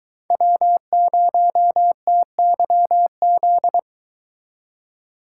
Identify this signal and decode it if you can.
Morse code, heard as a keyed tone, about 23 wpm, 700 Hz. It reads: W0TYZ